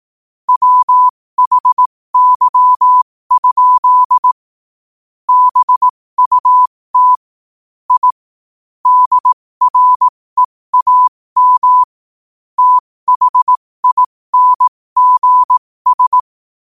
Morse code, heard as a keyed tone, about 18 wpm, spaced as for 15 wpm, 1000 Hz. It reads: WHY? BUT I DREAM THINGS
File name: qotdfiles / 15wpm